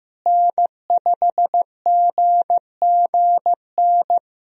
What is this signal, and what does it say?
Morse code, heard as a keyed tone, about 15 wpm, 700 Hz. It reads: N5GGN
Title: USA Call Signs - 5 Characters - Repeated 3x - 15wpm